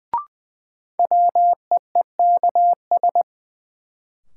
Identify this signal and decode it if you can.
Morse code, heard as a keyed tone, about 20 wpm, 700 Hz. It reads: WEEKS
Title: Sets of 5 Words - New Word Drill - 20wpm